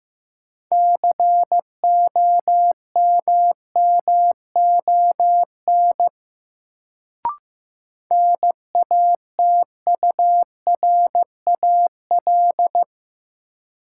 Morse code, heard as a keyed tone, about 15 wpm, 700 Hz. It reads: COMMON NATURAL
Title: Sentences from Top 700 Words - New Word Drill - 15wpm